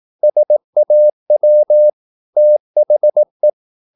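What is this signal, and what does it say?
Morse code, heard as a keyed tone, about 18 wpm, 600 Hz. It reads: SAW THE
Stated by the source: Morse practice at 18 wpm, English text, from CW med Gnister.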